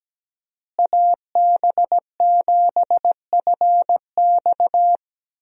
Morse code, heard as a keyed tone, about 17 wpm, 700 Hz. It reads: AB7FX
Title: USA Call Signs - 5 Characters - Repeated 3x - 17wpm